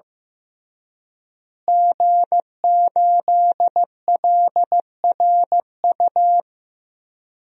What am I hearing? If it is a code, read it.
Morse code, heard as a keyed tone, about 16 wpm, 700 Hz. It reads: G8LRU
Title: DXCC-100 Call Signs - 5 Characters - Repeated 3x - 15wpm